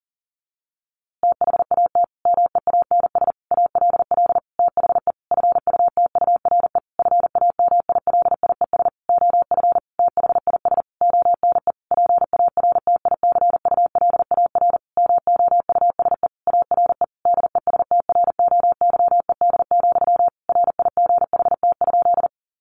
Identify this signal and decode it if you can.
Morse code, heard as a keyed tone, about 40 wpm, 700 Hz. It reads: THAT MEANS ALL THE FUTURE FAMILIES OF THIS ONE PARTICULAR MOUSE ARE DESTROYED, RIGHT?
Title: A Sound Of Thunder 40wpm